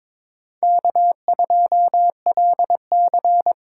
Morse code, heard as a keyed tone, about 22 wpm, 700 Hz. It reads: K2LC
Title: USA Call Signs - 4 Characters - 8x Character Spacing - 22wpm